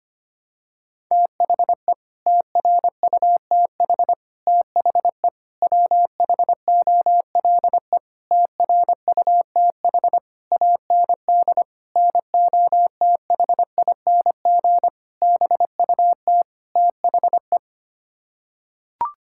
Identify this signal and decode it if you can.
Morse code, heard as a keyed tone, about 25 wpm, 700 Hz. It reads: THE TRUTH THE WHOLE TRUTH AND NOTHING BUT THE
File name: Sets of 9 Words - 25wpm